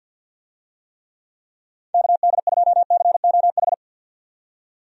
Morse code, heard as a keyed tone, about 50 wpm, 700 Hz. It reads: KD2COH
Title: USA Call Signs - 6 Characters - 50wpm